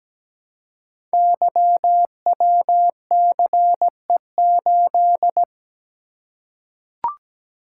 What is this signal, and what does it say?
Morse code, heard as a keyed tone, about 17 wpm, 700 Hz. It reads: YWCE8
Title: Code Groups - 5 Letter:Numbers - 17wpm